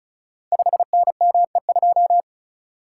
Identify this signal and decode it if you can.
Morse code, heard as a keyed tone, about 36 wpm, 700 Hz. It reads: 5NME2
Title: Code Groups - 5 Letter:Numbers - 35wpm